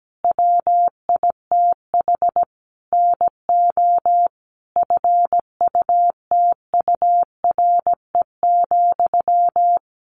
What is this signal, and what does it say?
Morse code, heard as a keyed tone, about 17 wpm, 700 Hz. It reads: WITH NO FUTURE,
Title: A Sound Of Thunder 17wpm